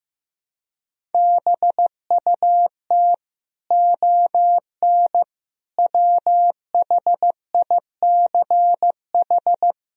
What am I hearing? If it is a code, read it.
Morse code, heard as a keyed tone, about 16 wpm, 700 Hz. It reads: BUT ON WHICH
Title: Sentences from Top 1000 Words - Review Part 1 - 15wpm